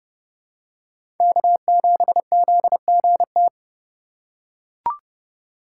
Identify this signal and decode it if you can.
Morse code, heard as a keyed tone, about 32 wpm, 700 Hz. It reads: K7ZGT
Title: USA Call Signs - 5 Characters - 4x Character Spacing - 30wpm